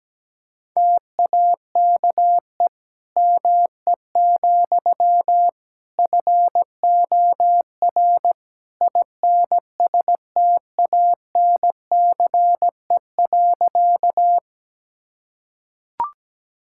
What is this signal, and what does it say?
Morse code, heard as a keyed tone, about 17 wpm, 700 Hz. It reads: TAKE ME, FOR INSTANCE.
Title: Sentences from Top 1000 Words - Review Part 5 - 17wpm